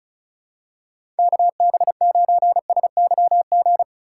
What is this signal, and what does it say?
Morse code, heard as a keyed tone, about 34 wpm, 700 Hz. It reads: KB9SYG
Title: USA Call Signs - 6 Characters - 2x Character Spacing - 35wpm